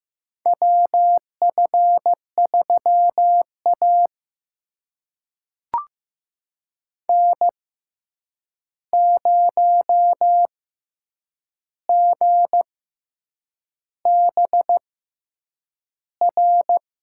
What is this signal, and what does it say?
Morse code, heard as a keyed tone, about 15 wpm, spaced as for 7 wpm, 700 Hz. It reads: WF3A N0GBR
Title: USA Call Signs - 6x Character Spacing - 15wpm